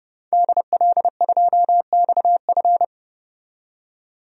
Morse code, heard as a keyed tone, about 32 wpm, 700 Hz. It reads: DL2XF
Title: DXCC-100 Call Signs - 5 Characters - 30wpm